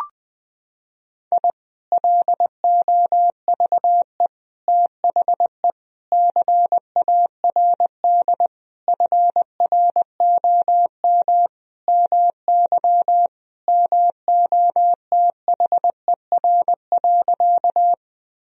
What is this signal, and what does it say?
Morse code, heard as a keyed tone, about 20 wpm, 700 Hz. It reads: I LOVE THE CARD FROM MY MOTHER.
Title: Sentences from Top 1000 Words - Review Part 4 - 20wpm